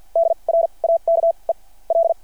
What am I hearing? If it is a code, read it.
Morse code, heard as a keyed tone, about 53 wpm, spaced as for 35 wpm, 690 Hz. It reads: DRAKE L
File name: archived / CRSnet